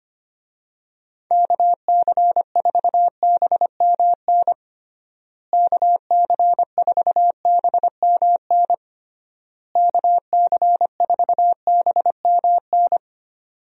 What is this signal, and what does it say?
Morse code, heard as a keyed tone, about 25 wpm, 700 Hz. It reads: KC4BMN KC4BMN KC4BMN
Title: USA Call Signs - 6 Characters - Repeated 3x - 25wpm